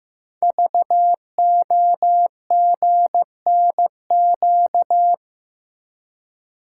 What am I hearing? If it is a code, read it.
Morse code, heard as a keyed tone, about 15 wpm, 700 Hz. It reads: VOGNQ
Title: Code Groups - 5 Letters - 15wpm